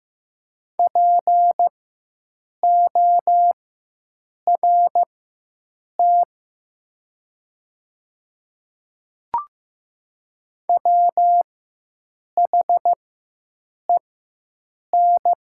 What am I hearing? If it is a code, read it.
Morse code, heard as a keyed tone, about 15 wpm, spaced as for 7 wpm, 700 Hz. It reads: PORT WHEN
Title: 4x Character Spacing - Top 200 Words - 15wpm